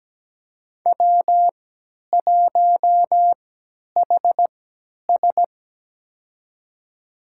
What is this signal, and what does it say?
Morse code, heard as a keyed tone, about 17 wpm, spaced as for 10 wpm, 700 Hz. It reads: W1HS
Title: USA Call Signs - 3x Character Spacing - 17wpm